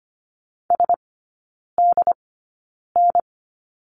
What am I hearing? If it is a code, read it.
Morse code, heard as a keyed tone, about 25 wpm, spaced as for 9 wpm, 700 Hz. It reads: SDN
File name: Lesson 15e - Number 5 - Three Characters 25wpm